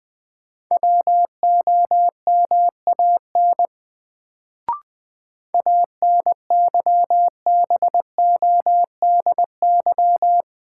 Morse code, heard as a keyed tone, about 20 wpm, 700 Hz. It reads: WOMAN ANYBODY
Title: Sets of 5 Words - New Word Drill - 20wpm